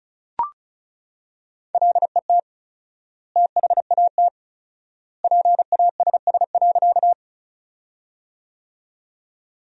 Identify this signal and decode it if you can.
Morse code, heard as a keyed tone, about 36 wpm, 700 Hz. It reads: LET THAT PASS.
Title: 4x Word-Spacing - Sentences from Top 400 Words - 35wpm